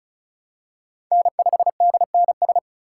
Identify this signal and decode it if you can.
Morse code, heard as a keyed tone, about 36 wpm, 700 Hz. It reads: N5DNS
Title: USA Call Signs - 5 Characters - 2x Character Spacing - 35wpm